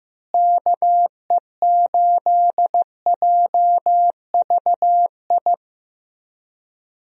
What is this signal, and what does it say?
Morse code, heard as a keyed tone, about 16 wpm, 700 Hz. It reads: KE8JVI
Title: USA Call Signs - 6 Characters - Repeated 2x - 15wpm